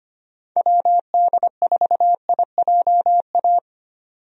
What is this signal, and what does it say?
Morse code, heard as a keyed tone, about 25 wpm, 700 Hz. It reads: WD4IJA
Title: USA Call Signs - 6 Characters - 25wpm